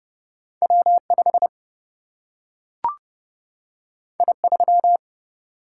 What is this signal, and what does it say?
Morse code, heard as a keyed tone, about 30 wpm, 700 Hz. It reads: W 5 I 3
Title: Single-Letter Single-Number 30wpm